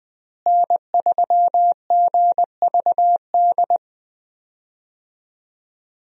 Morse code, heard as a keyed tone, about 20 wpm, 700 Hz. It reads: N3GVD
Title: USA Call Signs - 5 Characters - Repeated 2x - 20wpm